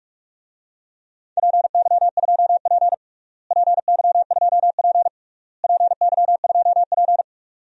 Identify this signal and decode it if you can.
Morse code, heard as a keyed tone, about 46 wpm, 700 Hz. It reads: PY2P PY2P PY2P
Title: DXCC-100 Call Signs - 4 Characters - Repeated 3x - 45wpm